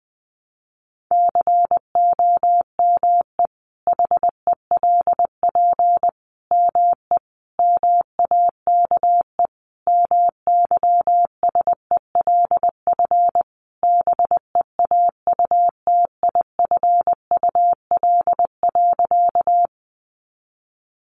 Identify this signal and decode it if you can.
Morse code, heard as a keyed tone, about 20 wpm, 700 Hz. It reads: COME HELP ME MAKE MYSELF BEAUTIFUL.
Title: Sentences from Top 1300 Words 20wpm